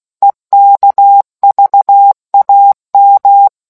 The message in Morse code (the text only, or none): EKVAM